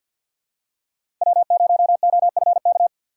Morse code, heard as a keyed tone, about 53 wpm, 700 Hz. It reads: W0OFK
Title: USA Call Signs - 5 Characters - 6x Character Spacing - 50wpm